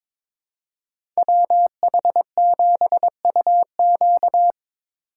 Morse code, heard as a keyed tone, about 22 wpm, 700 Hz. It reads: WH7UQ
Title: USA Call Signs - 5 Characters - 3x Character Spacing - 22wpm